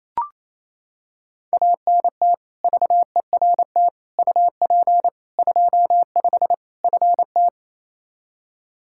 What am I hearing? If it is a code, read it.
Morse code, heard as a keyed tone, about 28 wpm, 700 Hz. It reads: ANT VERT UP 25 FT
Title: K1USN SST QSO Practice 28wpm